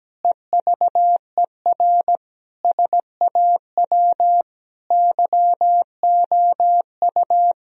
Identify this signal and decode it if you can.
Morse code, heard as a keyed tone, about 17 wpm, 700 Hz. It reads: EVER SAW YOU
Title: Sentences from Top 1000 Words - Review Part 4 - 17wpm